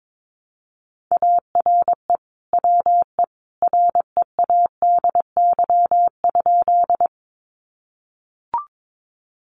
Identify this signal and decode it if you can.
Morse code, heard as a keyed tone, about 22 wpm, 700 Hz. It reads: ARE WE READY?
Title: Sentences from Top 400 Words 22wpm